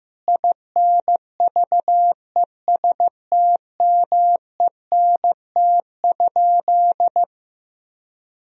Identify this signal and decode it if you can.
Morse code, heard as a keyed tone, about 16 wpm, 700 Hz. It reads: INVESTMENT?
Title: Sentences from Top 1000 Words - Review Part 4 - 15wpm